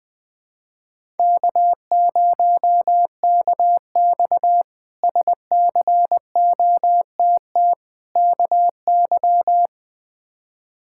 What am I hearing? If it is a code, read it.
Morse code, heard as a keyed tone, about 21 wpm, 700 Hz. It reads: K0KX SCOTT KY
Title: K1USN SST QSO Practice - 20wpm